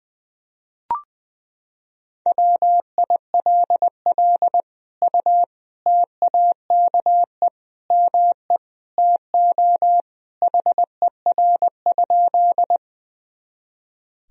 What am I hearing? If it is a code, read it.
Morse code, heard as a keyed tone, about 20 wpm, 700 Hz. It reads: WILL U TAKE ME TO HER?
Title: Sentences from Top 200 Words and Abbreviations 20wpm